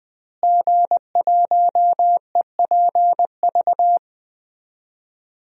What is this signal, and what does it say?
Morse code, heard as a keyed tone, about 20 wpm, 700 Hz. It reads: G1EPV
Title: DXCC-100 Call Signs - 5 Characters - Repeated 2x - 20wpm